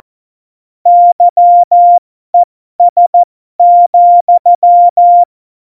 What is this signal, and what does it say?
Morse code, heard as a keyed tone, about 14 wpm, spaced as for 12 wpm, 700 Hz. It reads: YES,
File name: Morsekurs / Witze